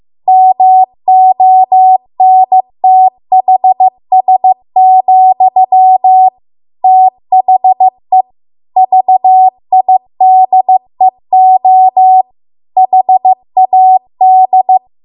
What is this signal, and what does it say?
Morse code, heard as a keyed tone, about 15 wpm, 750 Hz. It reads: MONTHS, THE VIDEO HAD